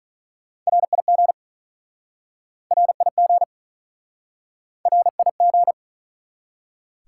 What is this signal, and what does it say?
Morse code, heard as a keyed tone, about 46 wpm, 700 Hz. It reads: RIG RIG RIG